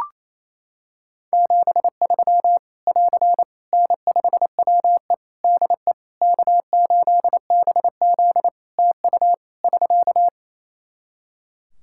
Morse code, heard as a keyed tone, about 27 wpm, 700 Hz. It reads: 73 <AR> N5WE DE K8BZ TU <SK>
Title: K1USN SST QSO Practice 28wpm